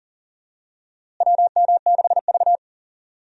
Morse code, heard as a keyed tone, about 40 wpm, 700 Hz. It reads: WM6V
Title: USA Call Signs - 4 Characters - Repeated 2x - 40wpm